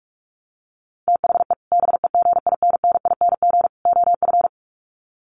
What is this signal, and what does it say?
Morse code, heard as a keyed tone, about 44 wpm, 700 Hz. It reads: THE BEGINNING OF
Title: Sets of 3 Words 45wpm